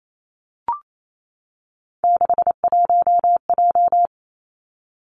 Morse code, heard as a keyed tone, about 28 wpm, 700 Hz. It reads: 61J
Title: Three Challenging Letter-Number Combinations 28wpm